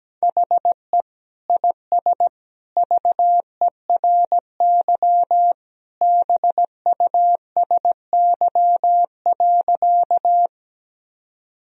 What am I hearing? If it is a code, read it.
Morse code, heard as a keyed tone, about 17 wpm, 700 Hz. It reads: HE IS VERY BUSY.
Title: Sentences from Top 1000 Words - Review Part 3 - 17wpm